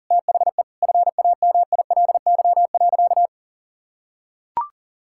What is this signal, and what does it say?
Morse code, heard as a keyed tone, about 40 wpm, 700 Hz. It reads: THE FAMILY.
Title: Sentences from Top 1000 Words - Review Part 5 - 40wpm